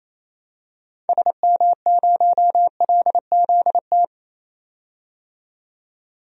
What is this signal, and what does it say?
Morse code, heard as a keyed tone, about 28 wpm, 700 Hz. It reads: SM0LZT
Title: DXCC-100 Call Signs - 6 Characters - 28wpm